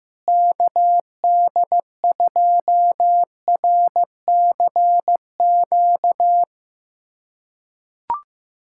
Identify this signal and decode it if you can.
Morse code, heard as a keyed tone, about 15 wpm, 700 Hz. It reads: KD2RCQ
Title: USA Callsign DE Callsign - 6 Characters - 15wpm